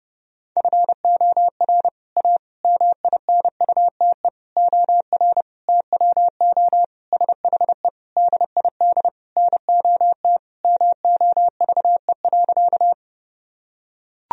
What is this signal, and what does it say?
Morse code, heard as a keyed tone, about 30 wpm, 700 Hz. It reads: FOR A MINUTE OR TWO SHE DID NOT MOVE.
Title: Sentences from Top 1000 Words - Review Part 1 - 30wpm